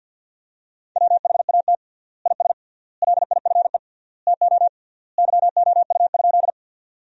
Morse code, heard as a keyed tone, about 47 wpm, 700 Hz. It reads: WHAT IS LIFE TO YOU?
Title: 3x Word-Spacing - Sentences from Top 300 Words - 50wpm